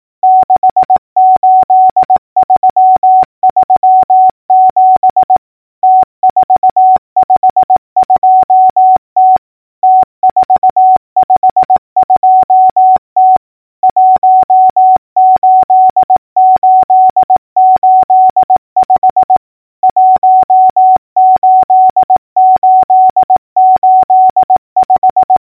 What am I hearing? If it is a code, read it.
Morse code, heard as a keyed tone, about 18 wpm, 750 Hz. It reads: 68337 T452T T452T 18885 18885
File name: numbers / V2-morse-clicks